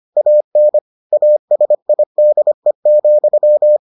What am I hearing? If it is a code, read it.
Morse code, heard as a keyed tone, about 25 wpm, 600 Hz. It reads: AN ASIDE,